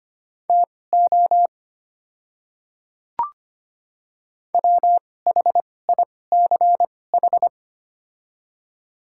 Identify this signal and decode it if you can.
Morse code, heard as a keyed tone, about 25 wpm, spaced as for 18 wpm, 700 Hz. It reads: TO WHICH
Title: 2x Character Spacing - Top 100 Words - 25wpm